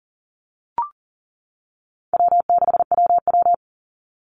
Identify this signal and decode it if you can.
Morse code, heard as a keyed tone, about 40 wpm, 700 Hz. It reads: W6WW
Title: NA SOTA Chaser Callsigns 40wpm